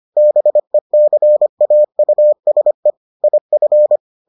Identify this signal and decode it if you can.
Morse code, heard as a keyed tone, about 25 wpm, 600 Hz. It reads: BECAUSE IF